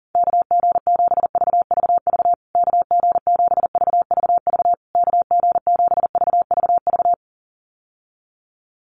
Morse code, heard as a keyed tone, about 40 wpm, 700 Hz. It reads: KG7VVV KG7VVV KG7VVV
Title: USA Call Signs Repeated 3x 40wpm